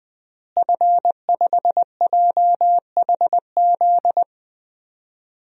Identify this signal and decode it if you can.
Morse code, heard as a keyed tone, about 20 wpm, 700 Hz. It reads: F5JHZ
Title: DXCC-100 Call Signs - 5 Characters - Repeated 2x - 20wpm